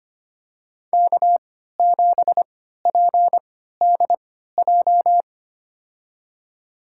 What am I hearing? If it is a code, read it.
Morse code, heard as a keyed tone, about 25 wpm, spaced as for 14 wpm, 700 Hz. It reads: K7PDJ
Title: USA Call Signs - 5 Characters - 3x Character Spacing - 25wpm